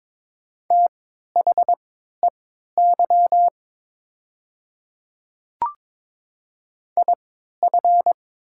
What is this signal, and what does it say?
Morse code, heard as a keyed tone, about 22 wpm, spaced as for 12 wpm, 700 Hz. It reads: THEY IF
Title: 3x Character Spacing - Top 100 Words - 22wpm